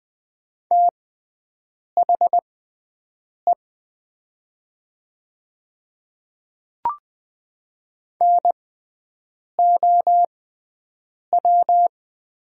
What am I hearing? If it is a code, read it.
Morse code, heard as a keyed tone, about 20 wpm, spaced as for 7 wpm, 700 Hz. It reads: THE NOW
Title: 6x Character Spacing - Top 100 Words - 20wpm